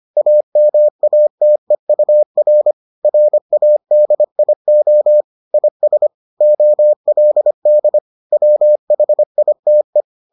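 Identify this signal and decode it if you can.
Morse code, heard as a keyed tone, about 25 wpm, 600 Hz. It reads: AMATEUR RADIO IS OLD WHITE
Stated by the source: Morse edition of the Foundations of Amateur Radio episodes.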